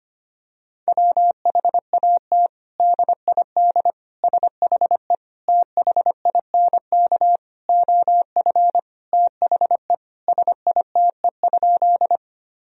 Morse code, heard as a keyed tone, about 25 wpm, 700 Hz. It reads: WHAT DID SHE THINK OF THE SITE?
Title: Sentences from Top 1000 Words - Review Part 4 - 25wpm